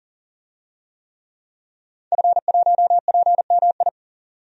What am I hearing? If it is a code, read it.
Morse code, heard as a keyed tone, about 40 wpm, 700 Hz. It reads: F1PMI
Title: DXCC-100 Call Signs - 5 Characters - 40wpm